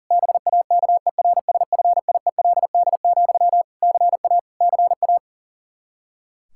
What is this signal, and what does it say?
Morse code, heard as a keyed tone, about 40 wpm, 700 Hz. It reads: BAKERSFIELD, CA CA
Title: Practice QSO - Exchange 1 - Beginning - 40wpm